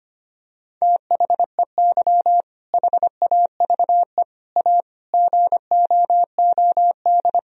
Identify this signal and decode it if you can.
Morse code, heard as a keyed tone, about 25 wpm, 700 Hz. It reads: THEY HAVE A GOOD
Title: Sentences from Top 1000 Words - Review Part 4 - 25wpm